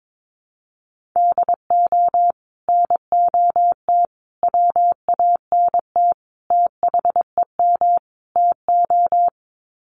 Morse code, heard as a keyed tone, about 22 wpm, 700 Hz. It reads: DO NOT WANT THEM TO
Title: Sets of 5 Words Encore 22wpm